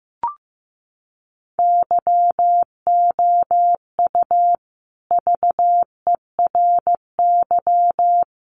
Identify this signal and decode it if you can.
Morse code, heard as a keyed tone, about 15 wpm, 700 Hz. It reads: YOU VERY
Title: Sets of 3 Words 15wpm